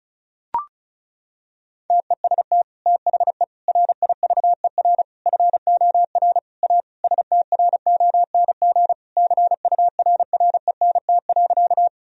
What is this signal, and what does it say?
Morse code, heard as a keyed tone, about 36 wpm, 700 Hz. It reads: TEST THE RIVER FOR A STRONG CURRENT.
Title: Sentences from Top 1000 Words - Review Part 4 - 35wpm